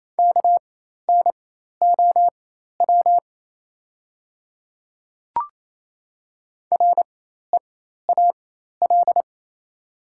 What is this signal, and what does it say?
Morse code, heard as a keyed tone, about 27 wpm, spaced as for 13 wpm, 690 Hz. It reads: KNOW REAL
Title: 4x Character Spacing - Top 300 Words - 28wpm